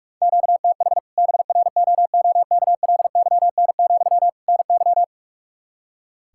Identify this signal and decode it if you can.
Morse code, heard as a keyed tone, about 46 wpm, 700 Hz. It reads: QTH BROOKLYN, NY
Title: Practice QSO - Exchange 1 - Beginning - 45wpm